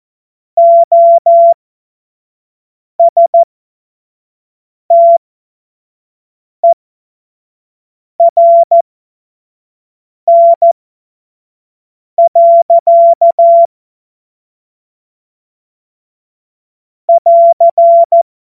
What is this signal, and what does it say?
Morse code, heard as a keyed tone, about 14 wpm, spaced as for 5 wpm, 680 Hz. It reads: OSTERN. <AR>
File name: Morsekurs / Witze / Witz 02. Mai 2023 25 F70